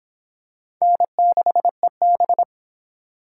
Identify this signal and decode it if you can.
Morse code, heard as a keyed tone, about 24 wpm, 700 Hz. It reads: N6EB